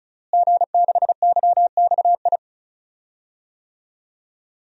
Morse code, heard as a keyed tone, about 34 wpm, 700 Hz. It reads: G6YXI
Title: DXCC-100 Call Signs - 5 Characters - Repeated 3x - 35wpm